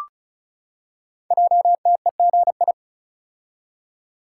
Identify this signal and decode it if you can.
Morse code, heard as a keyed tone, about 36 wpm, 700 Hz. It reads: JTEGI
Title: Code Groups - 5 Letters - 35wpm